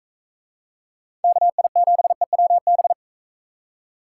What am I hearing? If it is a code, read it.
Morse code, heard as a keyed tone, about 39 wpm, 700 Hz. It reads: KI7EWB